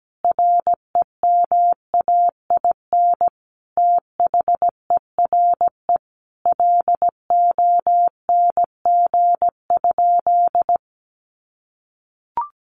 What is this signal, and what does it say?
Morse code, heard as a keyed tone, about 17 wpm, 700 Hz. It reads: REMAIN THERE LONG?
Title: Sentences from Top 1200 Words 17wpm